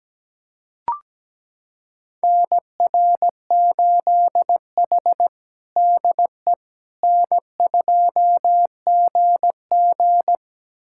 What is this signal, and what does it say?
Morse code, heard as a keyed tone, about 17 wpm, 700 Hz. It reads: NR8H DE N2GG
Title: USA Callsign DE Callsign - 4 Characters - 17wpm